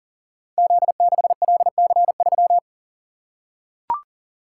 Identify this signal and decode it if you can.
Morse code, heard as a keyed tone, about 42 wpm, 700 Hz. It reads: Z6LC3
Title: Code Groups - 5 Letter:Numbers - 40wpm